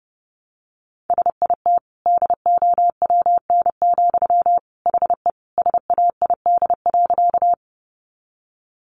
Morse code, heard as a keyed tone, about 30 wpm, 700 Hz. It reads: SIT DOWN, HE SAID.
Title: Sentences from Top 500 Words 30wpm